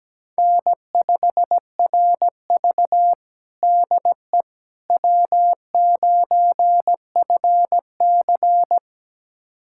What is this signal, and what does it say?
Morse code, heard as a keyed tone, about 17 wpm, 700 Hz. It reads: N5RV DE W9FC
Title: USA Callsign DE Callsign - 4 Characters - 17wpm